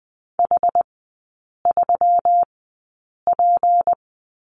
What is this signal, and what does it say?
Morse code, heard as a keyed tone, about 20 wpm, spaced as for 8 wpm, 700 Hz. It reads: H3P
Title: 159 Lesson 26e - Letter g - Three Characters 20wpm